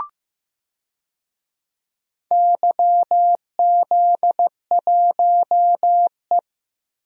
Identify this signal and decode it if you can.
Morse code, heard as a keyed tone, about 15 wpm, 700 Hz. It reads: YZ1E
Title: DXCC-100 Call Signs - 4 Characters - Repeated 2x - 15wpm